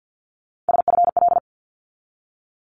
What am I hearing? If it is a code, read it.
Morse code, heard as a keyed tone, about 50 wpm, 700 Hz. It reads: SFL
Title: ARRL Sections 50wpm